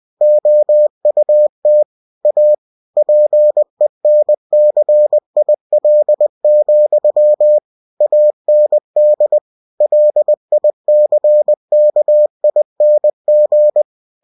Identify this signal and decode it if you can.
Morse code, heard as a keyed tone, about 20 wpm, 600 Hz. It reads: OUT A PENCIL, AND LICKING
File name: cw-books / winnie-the-pooh / winnie-20wpm-page-0062